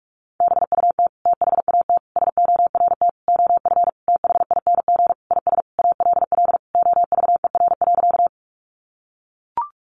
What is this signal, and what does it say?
Morse code, heard as a keyed tone, about 44 wpm, 700 Hz. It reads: BUT THAT SORT OF THING IS ALL OVER.
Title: Sentences from Top 1300 Words 45wpm